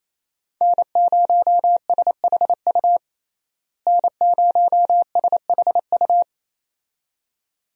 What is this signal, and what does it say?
Morse code, heard as a keyed tone, about 28 wpm, 700 Hz. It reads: N0SHU N0SHU
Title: USA Call Signs - 5 Characters - Repeated 2x - 28wpm